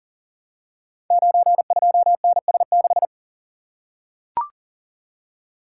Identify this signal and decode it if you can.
Morse code, heard as a keyed tone, about 39 wpm, 700 Hz. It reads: 92NS6
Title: Code Groups - 5 Letter:Numbers - 40wpm